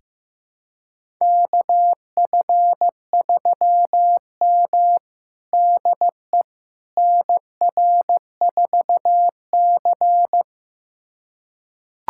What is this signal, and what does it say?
Morse code, heard as a keyed tone, about 16 wpm, 700 Hz. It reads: KF3M DE NR4C
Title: USA Callsign DE Callsign - 4 Characters - 15wpm